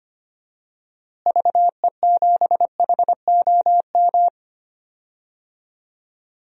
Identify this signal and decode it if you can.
Morse code, heard as a keyed tone, about 25 wpm, 700 Hz. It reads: VE7HOM
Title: DXCC-100 Call Signs - 6 Characters - Repeated 3x - 25wpm